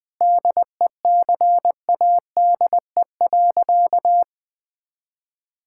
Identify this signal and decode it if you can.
Morse code, heard as a keyed tone, about 20 wpm, 700 Hz. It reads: DECADE.
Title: Sentences with 7-Word Phrases - 20wpm